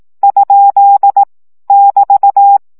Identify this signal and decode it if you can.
Morse code, heard as a keyed tone, about 18 wpm, 800 Hz. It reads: ? <BT>
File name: CW gyakorlas kerdojel 15 baud